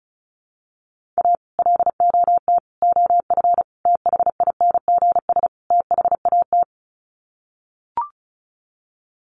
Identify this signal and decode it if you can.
Morse code, heard as a keyed tone, about 34 wpm, 700 Hz. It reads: A LOT OF THINGS THAT
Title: Sets of 5 Words 35wpm